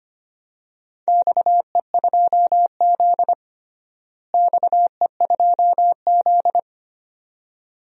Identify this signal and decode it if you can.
Morse code, heard as a keyed tone, about 25 wpm, 700 Hz. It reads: XE2Z XE2Z
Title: DXCC-100 Call Signs - 4 Characters - Repeated 2x - 25wpm